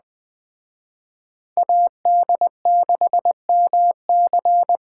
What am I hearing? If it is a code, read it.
Morse code, heard as a keyed tone, about 21 wpm, 700 Hz. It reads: AD6MC